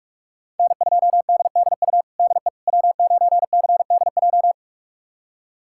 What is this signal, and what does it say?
Morse code, heard as a keyed tone, about 46 wpm, 700 Hz. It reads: N2DDU DE W9CDJ
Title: USA Callsign DE Callsign - 5 Characters - 45wpm